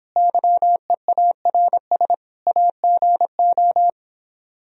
Morse code, heard as a keyed tone, about 26 wpm, 700 Hz. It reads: YEARS AGO